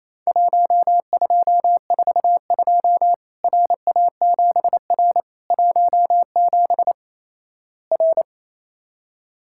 Morse code, heard as a keyed tone, about 28 wpm, 700 Hz. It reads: 1242 RA7R 17 R
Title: Worked All Europe DX Contest - QTC Exchange - 28wpm